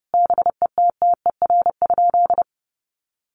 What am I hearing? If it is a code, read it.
Morse code, heard as a keyed tone, about 30 wpm, 700 Hz. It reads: BETTER?
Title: Sentences from Top 1500 Words 30wpm